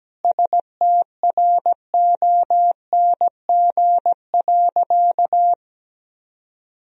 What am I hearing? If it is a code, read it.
Morse code, heard as a keyed tone, about 17 wpm, 700 Hz. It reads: STRONG.
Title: Sentences from Top 1000 Words - Review Part 4 - 17wpm